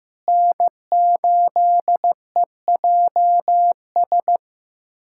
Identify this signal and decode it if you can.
Morse code, heard as a keyed tone, about 15 wpm, 700 Hz. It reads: N8EJS
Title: USA Callsign DE Callsign 15wpm